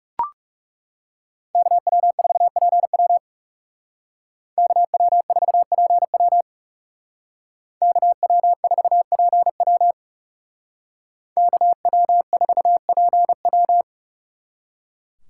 Morse code, heard as a keyed tone, about 38 wpm, 700 Hz. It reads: KW4PW KW4PW KW4PW KW4PW